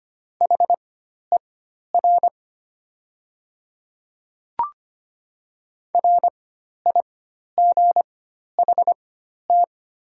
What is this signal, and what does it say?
Morse code, heard as a keyed tone, about 25 wpm, spaced as for 12 wpm, 690 Hz. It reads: HER RIGHT
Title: 4x Character Spacing - Top 500 Words - 25wpm